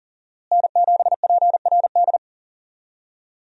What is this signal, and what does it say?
Morse code, heard as a keyed tone, about 39 wpm, 700 Hz. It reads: N7PRD
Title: USA Call Signs - 5 Characters - Repeated 2x - 40wpm